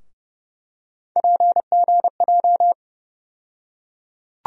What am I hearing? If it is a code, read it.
Morse code, heard as a keyed tone, about 30 wpm, 700 Hz. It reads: PGJ
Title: USA Call Sign Postfixes - 3 Letters 30wpm